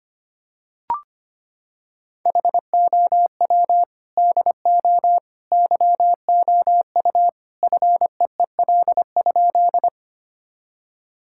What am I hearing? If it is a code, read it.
Morse code, heard as a keyed tone, about 25 wpm, 700 Hz. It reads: HOW DO YOU FEEL?
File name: practice / 25wpm / Sentences with 3-Word Phrases - 25wpm